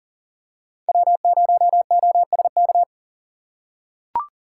Morse code, heard as a keyed tone, about 40 wpm, 700 Hz. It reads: W0OSK
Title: USA Call Signs - 5 Characters - 2x Character Spacing - 40wpm